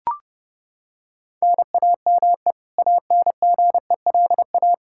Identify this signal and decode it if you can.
Morse code, heard as a keyed tone, about 30 wpm, 700 Hz. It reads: NAME ANGELA
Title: K1USN SST QSO Practice 30wpm